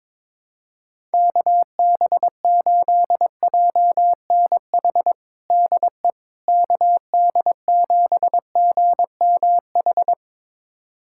Morse code, heard as a keyed tone, about 22 wpm, 700 Hz. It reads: KB8JNH DE KD7GMH
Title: USA Callsign DE Callsign - 6 Characters - 22wpm